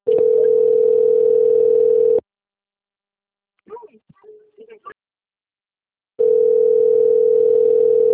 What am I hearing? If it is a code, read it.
Ringback tone (North America), 440 + 480 Hz.
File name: out- - - - - .56887